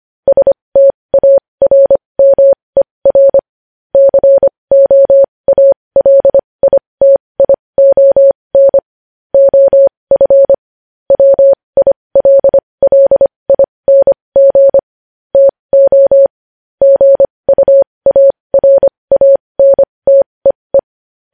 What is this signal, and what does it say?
Morse code, heard as a keyed tone, about 25 wpm, spaced as for 20 wpm, 550 Hz. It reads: STARMER COALITION OF WILLING TO GUARANTEE
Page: news-headlines-in-morse-code-at-20-wpm